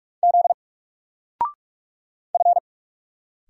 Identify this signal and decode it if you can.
Morse code, heard as a keyed tone, about 44 wpm, 700 Hz. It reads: Z F
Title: Single Letter-Number 45wpm